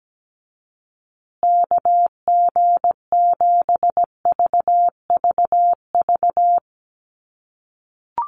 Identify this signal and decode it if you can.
Morse code, heard as a keyed tone, about 17 wpm, 700 Hz. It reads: KG7VVV
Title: USA Call Signs Repeated 3x 17wpm